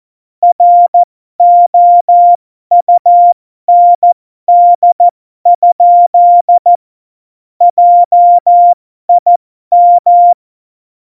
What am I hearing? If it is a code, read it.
Morse code, heard as a keyed tone, about 14 wpm, spaced as for 12 wpm, 700 Hz. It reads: ROUND? JIM
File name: Morsekurs / Witze